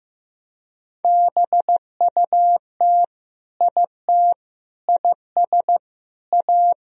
Morse code, heard as a keyed tone, about 15 wpm, 700 Hz. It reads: BUT IT IS A
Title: Sentences from Top 1000 Words - Review Part 3 - 15wpm